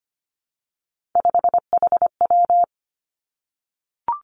Morse code, heard as a keyed tone, about 25 wpm, 700 Hz. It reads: 5HW
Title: Three Challenging Letter-Number Combinations 25wpm